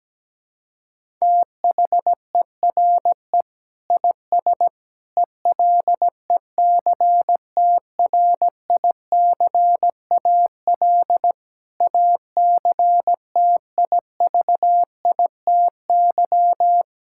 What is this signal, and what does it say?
Morse code, heard as a keyed tone, about 17 wpm, 700 Hz. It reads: THERE IS ELECTRICAL ACTIVITY
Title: Sentences from Top 1000 Words - Review Part 6 - 17wpm